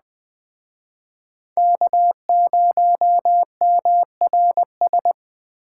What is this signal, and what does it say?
Morse code, heard as a keyed tone, about 20 wpm, 700 Hz. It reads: K0MRS
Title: Speed Racing - USA Call Signs 17wpm